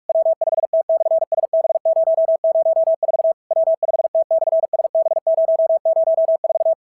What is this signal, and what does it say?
Morse code, heard as a keyed tone, about 44 wpm, 650 Hz. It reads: W5T/SB004 W5T/SB004
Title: SOTA QSO Practice 45wpm